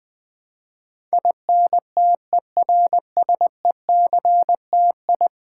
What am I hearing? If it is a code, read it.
Morse code, heard as a keyed tone, about 20 wpm, 700 Hz. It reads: INTERSECTI
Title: Build A Word - 5 Letters At A Time - 12-Letter Words - 20wpm